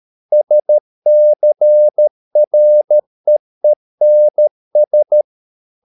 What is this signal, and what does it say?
Morse code, heard as a keyed tone, about 13 wpm, 600 Hz. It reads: SCREENS